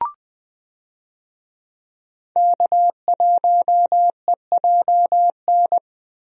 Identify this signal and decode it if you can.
Morse code, heard as a keyed tone, about 20 wpm, 700 Hz. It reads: K1EJN
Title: USA Call Signs - 5 Characters - 20wpm